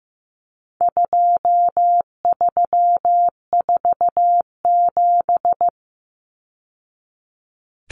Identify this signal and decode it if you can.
Morse code, heard as a keyed tone, about 15 wpm, 700 Hz. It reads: 2347
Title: Numbers - 4-Digit 15wpm